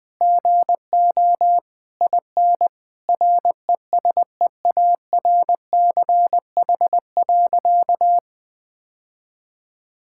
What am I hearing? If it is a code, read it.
Morse code, heard as a keyed tone, about 20 wpm, 700 Hz. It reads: GO IN RESEARCH.
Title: Sentences with 9-Word Phrases - 20wpm